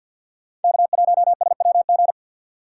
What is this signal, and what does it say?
Morse code, heard as a keyed tone, about 47 wpm, 700 Hz. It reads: K1SWG
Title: USA Call Signs - 3x Character Spacing - 50wpm